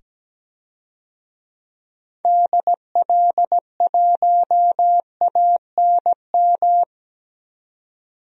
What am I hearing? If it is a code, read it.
Morse code, heard as a keyed tone, about 17 wpm, 700 Hz. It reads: DL1ANM
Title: DXCC-100 Call Signs - 6 Characters - 17wpm